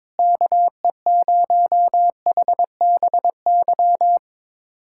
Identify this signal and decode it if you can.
Morse code, heard as a keyed tone, about 22 wpm, 700 Hz. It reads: KE0HBY
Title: USA Callsign DE Callsign - 6 Characters - 22wpm